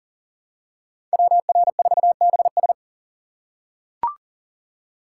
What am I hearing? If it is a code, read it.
Morse code, heard as a keyed tone, about 40 wpm, 700 Hz. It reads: WR4BS
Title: USA Call Signs - 5 Characters - 3x Character Spacing - 40wpm